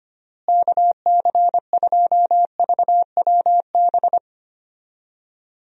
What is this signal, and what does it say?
Morse code, heard as a keyed tone, about 25 wpm, 700 Hz. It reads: KC2VWB
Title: USA Call Signs - 6 Characters - Repeated 3x - 25wpm